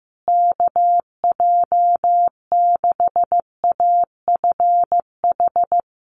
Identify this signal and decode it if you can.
Morse code, heard as a keyed tone, about 15 wpm, 700 Hz. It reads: KJ6AFH
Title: USA Call Signs Repeated 3x 15wpm